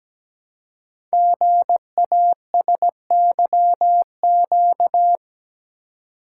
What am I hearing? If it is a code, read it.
Morse code, heard as a keyed tone, about 17 wpm, 700 Hz. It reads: GASYQ
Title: Code Groups - 5 Letter:Numbers - 17wpm